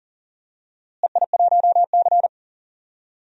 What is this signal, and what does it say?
Morse code, heard as a keyed tone, about 40 wpm, 700 Hz. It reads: EI1C
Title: DXCC-100 Call Signs - 4 Characters - Repeated 3x - 40wpm